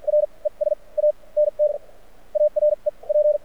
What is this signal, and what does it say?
Morse code, heard as a keyed tone, about 53 wpm, spaced as for 25 wpm, 600 Hz. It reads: RE SANTA REP
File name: archived / CRSnet